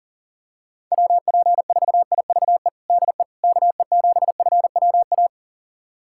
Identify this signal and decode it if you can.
Morse code, heard as a keyed tone, about 40 wpm, 700 Hz. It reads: WP4IVE DE KE7FWA
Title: USA Callsign DE Callsign 40wpm